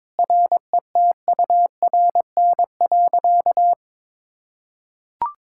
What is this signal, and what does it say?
Morse code, heard as a keyed tone, about 22 wpm, 700 Hz. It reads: RETURN.
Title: Sentences from Top 1000 Words - Review Part 6 - 22wpm